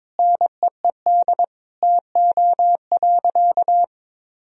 Morse code, heard as a keyed tone, about 22 wpm, 700 Hz. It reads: NEED TO.
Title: Sentences from Top 1000 Words - Review Part 3 - 22wpm